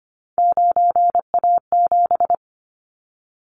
Morse code, heard as a keyed tone, about 25 wpm, 700 Hz. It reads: 9A7
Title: Callsign Prefixes for Top 25 DXCC - Repeated 3x - 25wpm